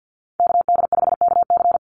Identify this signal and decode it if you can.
Morse code, heard as a keyed tone, about 50 wpm, 700 Hz. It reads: KD5KC
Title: NA SOTA Chaser Callsigns 50wpm